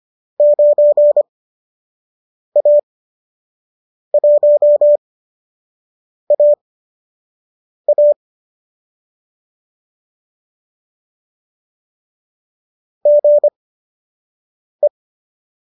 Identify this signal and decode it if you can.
Morse code, heard as a keyed tone, about 25 wpm, spaced as for 6 wpm, 590 Hz. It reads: 9A1AA GE
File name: qso012 06